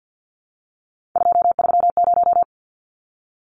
Morse code, heard as a keyed tone, about 50 wpm, 700 Hz. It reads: 230
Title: Numbers - 3-Digit 50wpm